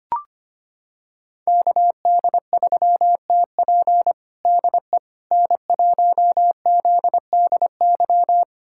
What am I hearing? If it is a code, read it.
Morse code, heard as a keyed tone, about 25 wpm, 700 Hz. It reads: KD3TP DE N1ZDY
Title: USA Callsign DE Callsign - 5 Characters - 25wpm